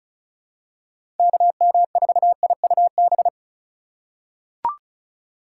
Morse code, heard as a keyed tone, about 34 wpm, 700 Hz. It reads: KM4IUB
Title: USA Call Signs - 6 Characters - 2x Character Spacing - 35wpm